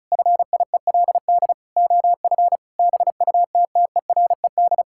Morse code, heard as a keyed tone, about 34 wpm, 700 Hz. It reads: FIELD OF BUTTERED
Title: Pangram Challenge v2 - 35wpm